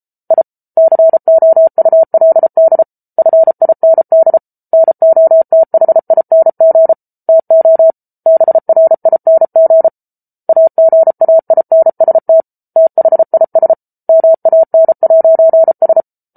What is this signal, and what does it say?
Morse code, heard as a keyed tone, about 34 wpm, 650 Hz. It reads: I COULD FIND NOTHING TO BRING AGAINST THIS MAN'S